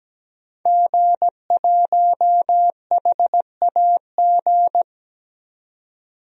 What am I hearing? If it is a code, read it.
Morse code, heard as a keyed tone, about 17 wpm, 700 Hz. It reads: G1HAG
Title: DXCC-100 Call Signs - 5 Characters - 17wpm